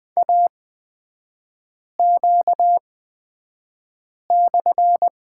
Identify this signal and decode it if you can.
Morse code, heard as a keyed tone, about 20 wpm, spaced as for 5 wpm, 700 Hz. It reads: AQ/
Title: 255 Lesson 41e - Period - Three Characters 20wpm